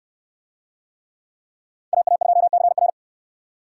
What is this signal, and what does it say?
Morse code, heard as a keyed tone, about 69 wpm, 700 Hz. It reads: AA2CR